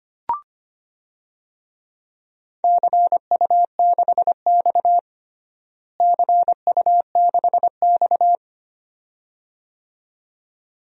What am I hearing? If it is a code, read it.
Morse code, heard as a keyed tone, about 25 wpm, 700 Hz. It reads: CU6X CU6X
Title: DXCC-100 Call Signs - 4 Characters - Repeated 2x - 25wpm